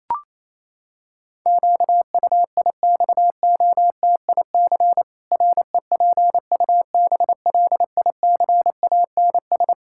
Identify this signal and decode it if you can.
Morse code, heard as a keyed tone, about 28 wpm, 700 Hz. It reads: QUIXOTIC REPUBLICANS
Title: Pangram Challenge v2 - 28wpm